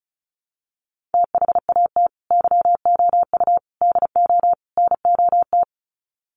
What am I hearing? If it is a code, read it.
Morse code, heard as a keyed tone, about 34 wpm, 700 Hz. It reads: THAT YOU DO NOT
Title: Sets of 4 Words 35wpm